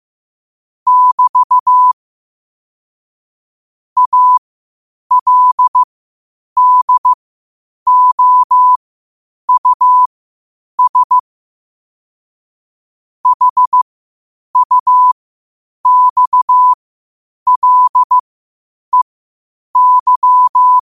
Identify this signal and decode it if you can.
Morse code, heard as a keyed tone, about 15 wpm, spaced as for 8 wpm, 1000 Hz. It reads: <BT> ALDOUS HUXLEY
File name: qotdfiles / 8wpm